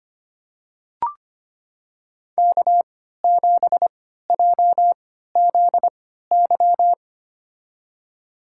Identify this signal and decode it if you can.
Morse code, heard as a keyed tone, about 25 wpm, spaced as for 14 wpm, 700 Hz. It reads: K7JZY
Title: USA Call Signs - 3x Character Spacing - 25wpm